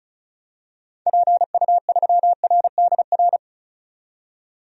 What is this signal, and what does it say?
Morse code, heard as a keyed tone, about 36 wpm, 690 Hz. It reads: PU3RDR
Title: DXCC-100 Call Signs - 6 Characters - 35wpm